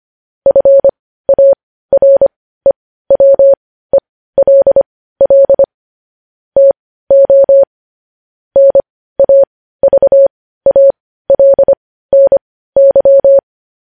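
Morse code, heard as a keyed tone, about 25 wpm, spaced as for 15 wpm, 550 Hz. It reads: FAREWELL TO NAVALNY
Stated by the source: Morse code practice of news headlines at 15 wpm, opening with vvv.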